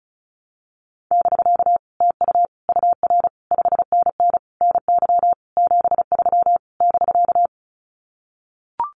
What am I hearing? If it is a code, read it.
Morse code, heard as a keyed tone, about 34 wpm, 700 Hz. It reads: <BK> TU UR 5NN NY 73 <BK>
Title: Common QSO Phrases Encore 35wpm